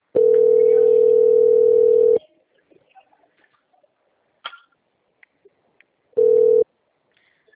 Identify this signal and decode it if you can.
Ringback tone (North America), 440 + 480 Hz.
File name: out- - - - - .53391